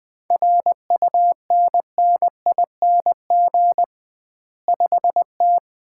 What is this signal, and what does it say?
Morse code, heard as a keyed tone, about 21 wpm, 700 Hz. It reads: RUNNING 5T
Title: Common QSO Phrases - 2x Word-Spacing - 20wpm